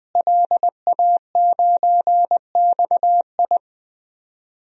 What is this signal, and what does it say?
Morse code, heard as a keyed tone, about 21 wpm, 700 Hz. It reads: LA9XI
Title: DXCC-100 Call Signs - 5 Characters - Repeated 3x - 20wpm